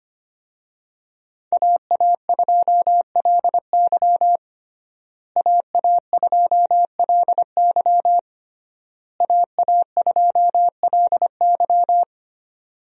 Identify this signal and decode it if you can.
Morse code, heard as a keyed tone, about 25 wpm, 700 Hz. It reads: AA2LY AA2LY AA2LY
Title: USA Call Signs - 5 Characters - Repeated 3x - 25wpm